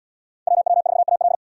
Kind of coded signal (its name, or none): Morse code